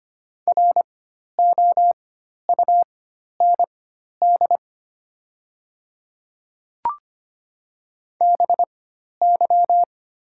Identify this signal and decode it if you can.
Morse code, heard as a keyed tone, about 25 wpm, spaced as for 12 wpm, 700 Hz. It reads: ROUND BY
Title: 4x Character Spacing - Top 300 Words - 25wpm